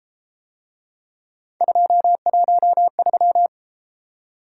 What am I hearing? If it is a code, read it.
Morse code, heard as a keyed tone, about 30 wpm, 700 Hz. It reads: 213